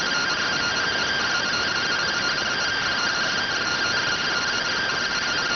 Two-tone teleprinter signal, about 170 Hz shift.